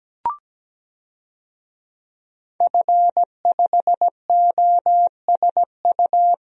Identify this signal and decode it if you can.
Morse code, heard as a keyed tone, about 17 wpm, 700 Hz. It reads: F5OSU
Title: DXCC-100 Call Signs - 5 Characters - Repeated 2x - 17wpm